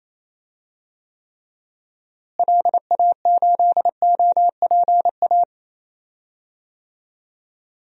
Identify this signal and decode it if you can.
Morse code, heard as a keyed tone, about 28 wpm, 700 Hz. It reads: LA8OPA
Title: DXCC-100 Call Signs - 6 Characters - Repeated 2x - 28wpm